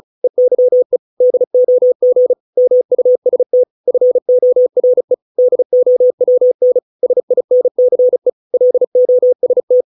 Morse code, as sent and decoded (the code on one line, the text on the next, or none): . -.-- . / -.. --- --. / -- ..- ... - / ..-. --- .-. / -.. --- .-- -. / ... .. -. -.-. / .-.. --- ... -
EYE DOG MUST FORE DOWN SINCE LOST